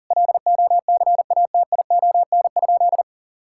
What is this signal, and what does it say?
Morse code, heard as a keyed tone, about 40 wpm, 690 Hz. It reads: LOCATION?
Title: Sentences from Top 1000 Words - Review Part 6 - 40wpm